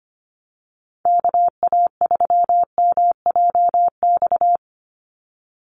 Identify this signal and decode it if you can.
Morse code, heard as a keyed tone, about 25 wpm, 700 Hz. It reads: KA3MJX
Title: USA Call Signs 25wpm